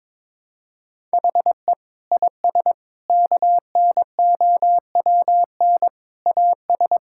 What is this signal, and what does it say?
Morse code, heard as a keyed tone, about 22 wpm, 700 Hz. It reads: HE IS KNOWN AS
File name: practice / 22wpm / Sentences with 4-Word Phrases - 22wpm